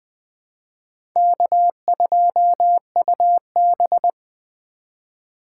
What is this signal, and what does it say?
Morse code, heard as a keyed tone, about 21 wpm, 700 Hz. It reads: K2UB
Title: Speed Racing - USA 1x2 Call Signs 17wpm